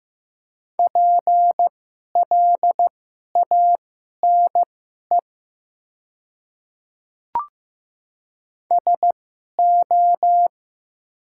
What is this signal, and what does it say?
Morse code, heard as a keyed tone, about 15 wpm, spaced as for 11 wpm, 690 Hz. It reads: PLANE SO
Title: 2x Character Spacing - Top 500 Words - 15wpm